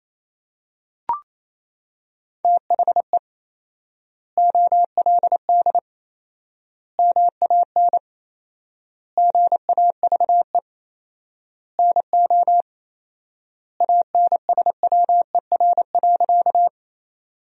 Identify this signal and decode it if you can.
Morse code, heard as a keyed tone, about 28 wpm, 700 Hz. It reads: THE OLD MAN GAVE NO ANSWER.
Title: 4x Word-Spacing - Sentences from Top 500 Words - 28wpm